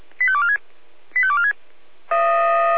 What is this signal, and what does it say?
Five-tone selective call (ZVEI), heard as quick runs of five tones, tones of about 70 ms. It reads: 85246 85246